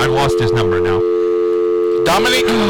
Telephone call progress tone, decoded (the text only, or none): dial tone